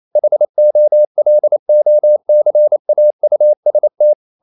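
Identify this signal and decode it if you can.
Morse code, heard as a keyed tone, about 28 wpm, 600 Hz. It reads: HOLOCAUST
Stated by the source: Morse code practice at 28 wpm, text of Ken Follett's novel Never in English, from CW med Gnister.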